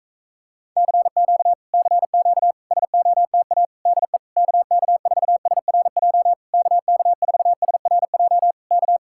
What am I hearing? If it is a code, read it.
Morse code, heard as a keyed tone, about 44 wpm, 700 Hz. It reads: CQ CQ SOTA DE KK4SRJ KK4SRJ K